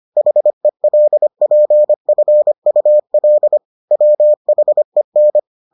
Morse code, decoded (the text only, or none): HELPFUL WHEN